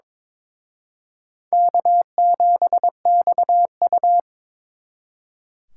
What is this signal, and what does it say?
Morse code, heard as a keyed tone, about 26 wpm, 700 Hz. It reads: K7XU
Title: Speed Racing - USA 1x2 Call Signs 22wpm